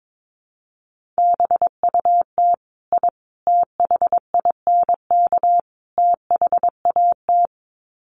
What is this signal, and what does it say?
Morse code, heard as a keyed tone, about 22 wpm, 700 Hz. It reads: BUT I THINK THAT
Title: Sets of 4 Words 22wpm